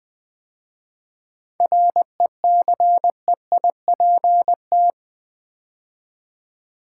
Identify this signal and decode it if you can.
Morse code, heard as a keyed tone, about 20 wpm, 700 Hz. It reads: RECEIPT
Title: 7 Letter Words - First 3 Letters Given - 20wpm